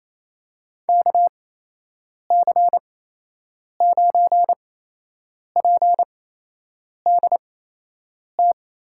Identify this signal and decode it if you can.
Morse code, heard as a keyed tone, about 28 wpm, spaced as for 8 wpm, 700 Hz. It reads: KC9PDT
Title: USA Call Signs - 8x Character Spacing - 28wpm